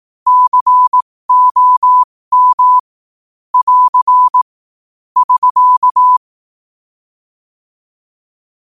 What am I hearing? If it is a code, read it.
Morse code, heard as a keyed tone, about 18 wpm, spaced as for 15 wpm, 1000 Hz. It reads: COM <AR> <SK>
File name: qotdfiles / 15wpm